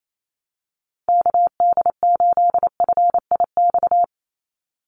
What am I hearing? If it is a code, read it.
Morse code, heard as a keyed tone, about 28 wpm, 700 Hz. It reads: KD8FIX
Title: USA Call Signs 28wpm